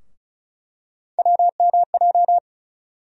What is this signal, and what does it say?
Morse code, heard as a keyed tone, about 36 wpm, 700 Hz. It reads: WMJ
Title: USA Call Sign Postfixes - 3 Letters 35wpm